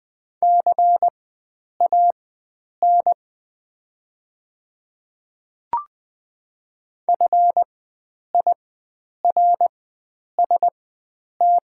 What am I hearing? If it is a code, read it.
Morse code, heard as a keyed tone, about 20 wpm, spaced as for 9 wpm, 700 Hz. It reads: CAN FIRST
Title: 4x Character Spacing - Top 100 Words - 20wpm